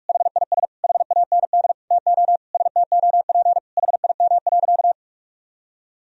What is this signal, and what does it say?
Morse code, heard as a keyed tone, about 41 wpm, 700 Hz. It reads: HIS HAND TO STOP HIM.
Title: Sentences from Top 1000 Words - Review Part 4 - 45wpm